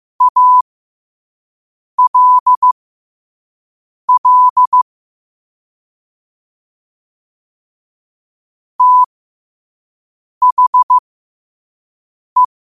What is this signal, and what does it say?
Morse code, heard as a keyed tone, about 15 wpm, spaced as for 5 wpm, 1000 Hz. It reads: ALL THE
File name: qotdfiles / 5wpm